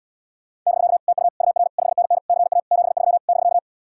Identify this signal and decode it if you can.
Morse code, heard as a keyed tone, about 79 wpm, 700 Hz. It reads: <BK> TU UR 5NN CA 73 <BK>